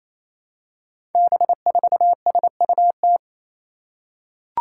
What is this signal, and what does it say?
Morse code, heard as a keyed tone, about 28 wpm, 700 Hz. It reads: B4SUT
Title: Five Challenging Letter-Number Combinations 28wpm